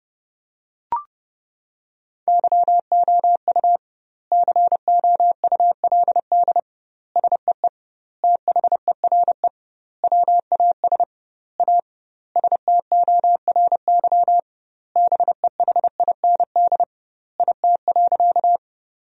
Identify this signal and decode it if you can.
Morse code, heard as a keyed tone, about 32 wpm, 700 Hz. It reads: YOU COULD SEE THERE WAS A STORY BEHIND IT.
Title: 2x Word-Spacing - Sentences from Top 500 Words - 30wpm